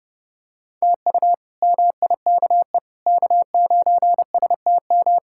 Morse code, heard as a keyed tone, about 32 wpm, 700 Hz. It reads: TU MIKE K9STM
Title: K1USN SST QSO Practice - 30wpm